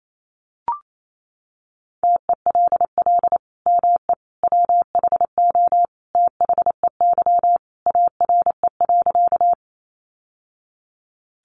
Morse code, heard as a keyed tone, about 28 wpm, 700 Hz. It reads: TELL ME WHO THEY ARE.
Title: Sentences from Top 700 Words 28wpm